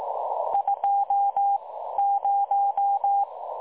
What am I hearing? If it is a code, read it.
Morse code, heard as a keyed tone, about 18 wpm, spaced as for 13 wpm, 810 Hz. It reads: 20
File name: dlazaawansowanych / nagraniazpasma / websdr recording 3565.0kHz